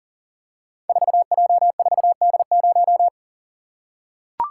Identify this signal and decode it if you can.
Morse code, heard as a keyed tone, about 41 wpm, 700 Hz. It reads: 4J4D0
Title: Code Groups - 5 Letter:Numbers - 40wpm